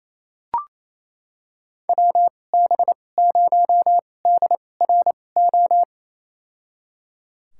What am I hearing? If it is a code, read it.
Morse code, heard as a keyed tone, about 28 wpm, spaced as for 20 wpm, 690 Hz. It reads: WB0DRO
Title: USA Call Signs - 6 Characters - 2x Character Spacing - 28wpm